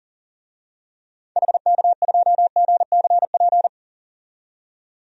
Morse code, heard as a keyed tone, about 40 wpm, 700 Hz. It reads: HK2GCP
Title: DXCC-100 Call Signs - 6 Characters - 40wpm